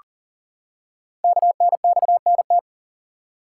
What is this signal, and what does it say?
Morse code, heard as a keyed tone, about 39 wpm, 690 Hz. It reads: KNXNT
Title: Code Groups - 5 Letters - 40wpm